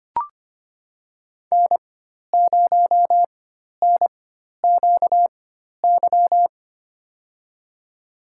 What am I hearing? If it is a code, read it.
Morse code, heard as a keyed tone, about 25 wpm, spaced as for 12 wpm, 700 Hz. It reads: N0NQY
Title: USA Call Signs - 5 Characters - 4x Character Spacing - 25wpm